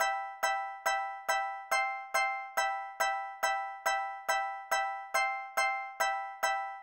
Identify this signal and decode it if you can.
There is no code.